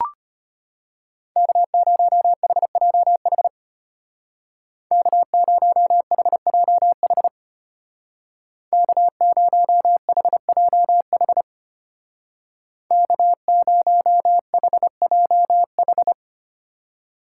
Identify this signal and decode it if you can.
Morse code, heard as a keyed tone, about 30 wpm, 700 Hz. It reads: K0HJH K0HJH K0HJH K0HJH
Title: Speed Racing - USA 1x3 Call Signs 25wpm